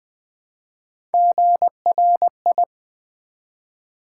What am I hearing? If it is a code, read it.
Morse code, heard as a keyed tone, about 20 wpm, 700 Hz. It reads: GRI
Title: Build A Word - 1 Letter At A Time - 4-Letter Words - 20wpm